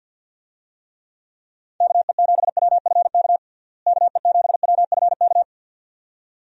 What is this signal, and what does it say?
Morse code, heard as a keyed tone, about 51 wpm, 700 Hz. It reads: KE7WFK KE7WFK
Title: USA Call Signs - 6 Characters - Repeated 2x - 50wpm